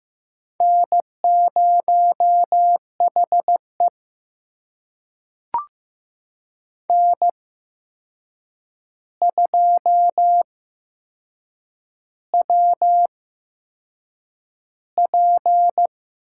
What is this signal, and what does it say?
Morse code, heard as a keyed tone, about 15 wpm, spaced as for 6 wpm, 700 Hz. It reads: N0HE N2WP
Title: USA Call Signs - 4 Characters - 8x Character Spacing - 15wpm